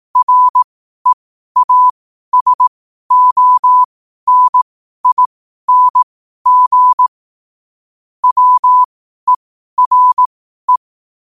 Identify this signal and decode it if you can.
Morse code, heard as a keyed tone, about 18 wpm, spaced as for 12 wpm, 1000 Hz. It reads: REASONING WERE
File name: qotdfiles / 12wpm